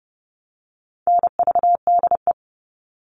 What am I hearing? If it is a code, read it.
Morse code, heard as a keyed tone, about 30 wpm, 700 Hz. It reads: NVDE
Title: All Cut Numbers - 4-Digit 30wpm